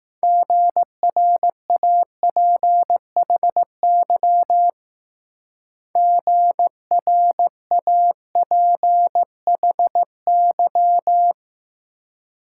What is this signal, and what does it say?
Morse code, heard as a keyed tone, about 19 wpm, 700 Hz. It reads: GRAPHY GRAPHY